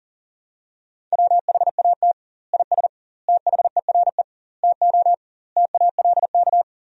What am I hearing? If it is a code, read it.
Morse code, heard as a keyed tone, about 40 wpm, 700 Hz. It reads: WHAT IS THERE TO TALK
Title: 2x Word-Spacing - Sentences from Top 400 Words - 40wpm